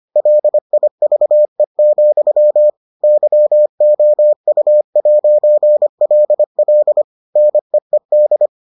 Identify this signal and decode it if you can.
Morse code, heard as a keyed tone, about 25 wpm, 600 Hz. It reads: LIVE, YOU'LL NEED